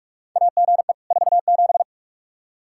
Morse code, heard as a keyed tone, about 46 wpm, 700 Hz. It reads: AGE 47
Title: K1USN SST QSO Practice 45wpm